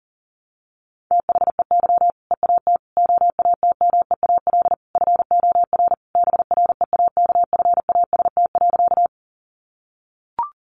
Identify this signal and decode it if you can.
Morse code, heard as a keyed tone, about 37 wpm, 700 Hz. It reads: THEY EAT OATMEAL FOR BREAKFAST.
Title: Easy Phrases 40wpm